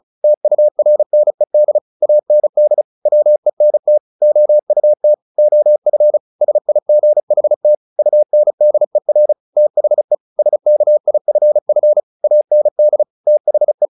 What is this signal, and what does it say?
Morse code, heard as a keyed tone, about 34 wpm, 600 Hz. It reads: TURNED AND WENT OUT OF SIGHT UNDER THE SKIFF AND THE
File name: the old man and the sea 2 35wpm